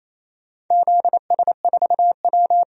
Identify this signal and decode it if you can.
Morse code, heard as a keyed tone, about 28 wpm, 700 Hz. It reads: ZS4W
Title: DXCC-100 Call Signs - 4 Characters - 28wpm